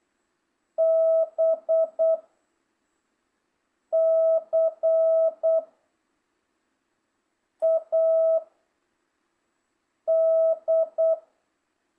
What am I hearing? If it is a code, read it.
Morse code, heard as a keyed tone, about 8 wpm, 1300 Hz. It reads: B C A D